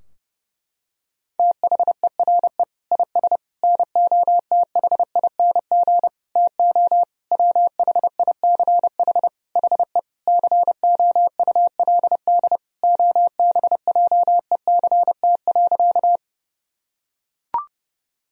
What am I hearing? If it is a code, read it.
Morse code, heard as a keyed tone, about 32 wpm, 700 Hz. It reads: THERE IS NOTHING TO WHICH HE COULD OBJECT.
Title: Sentences from Top 1000 Words - Review Part 3 - 30wpm